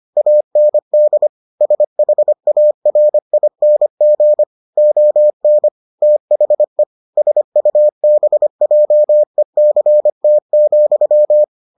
Morse code, heard as a keyed tone, about 25 wpm, 600 Hz. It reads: AND SHARING ON THE SUBJECT,